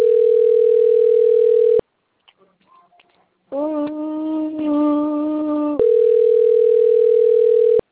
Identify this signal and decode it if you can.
Ringback tone (North America), 440 + 480 Hz.
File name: rg-1600-unknown- - - .48867